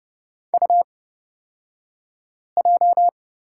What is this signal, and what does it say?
Morse code, heard as a keyed tone, about 29 wpm, spaced as for 5 wpm, 690 Hz. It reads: UJ
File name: Lesson 36d - BT - Two Characters 30wpm